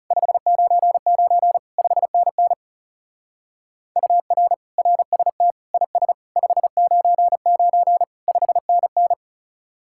Morse code, heard as a keyed tone, about 37 wpm, 700 Hz. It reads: 599 5NN UR RST IS 599 5NN
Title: Speed-Racing Common QSO Phrases - 30wpm